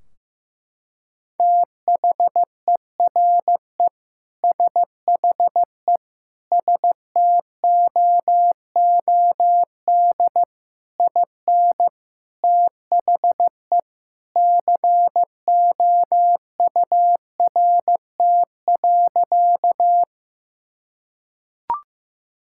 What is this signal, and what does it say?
Morse code, heard as a keyed tone, about 15 wpm, 700 Hz. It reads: THERE SHE STOOD IN THE COURT.
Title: Sentences from Top 1600 Words 15wpm